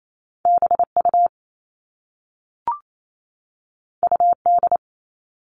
Morse code, heard as a keyed tone, about 28 wpm, 700 Hz. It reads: BU UD
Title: All Cut Numbers - 2-Digit 28wpm